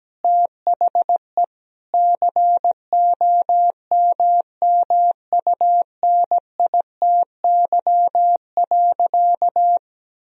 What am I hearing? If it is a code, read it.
Morse code, heard as a keyed tone, about 17 wpm, 700 Hz. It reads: THE COMMUNITY.
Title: Sentences from Top 1000 Words - Review Part 4 - 17wpm